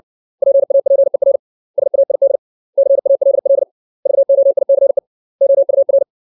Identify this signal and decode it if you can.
Morse code, heard as a keyed tone, about 60 wpm, 550 Hz. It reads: PAPER SAID CALL VOICE OUR
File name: 632 words / 632 Words 60WPM